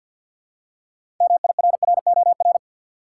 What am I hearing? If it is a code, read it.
Morse code, heard as a keyed tone, about 47 wpm, 700 Hz. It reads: MIRROR
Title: Six Letter Words - 50wpm